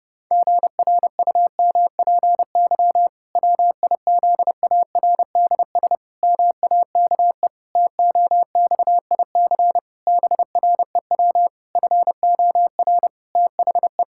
Morse code, heard as a keyed tone, about 32 wpm, 700 Hz. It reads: GRUMPY WIZARDS MAKE TOXIC BREW FOR THE
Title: Pangram Challenge v2 - 30wpm